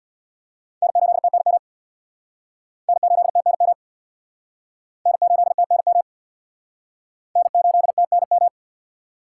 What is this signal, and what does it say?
Morse code, heard as a keyed tone, about 61 wpm, 700 Hz. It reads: N8TNM N8TNM N8TNM N8TNM
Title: Speed Racing - USA 1x3 Call Signs 50wpm